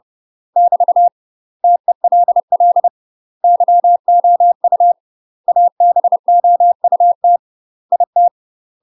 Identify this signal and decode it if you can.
Morse code, heard as a keyed tone, about 30 wpm, 700 Hz. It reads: <BT> TELL YOU ABOUT IT
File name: UK MASTER QSOs TRAINING / 30WPM / UK MASTER QSOs 0024